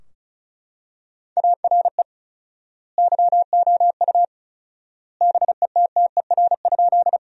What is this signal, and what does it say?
Morse code, heard as a keyed tone, about 34 wpm, 700 Hz. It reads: ARE YOU BETTER?
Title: 4x Word-Spacing - Sentences from Top 400 Words - 35wpm